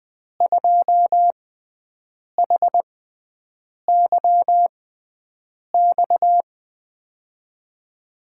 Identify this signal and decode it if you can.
Morse code, heard as a keyed tone, about 20 wpm, spaced as for 7 wpm, 700 Hz. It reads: 2HYX
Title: USA Call Signs - 6 Characters - 6x Character Spacing - 20wpm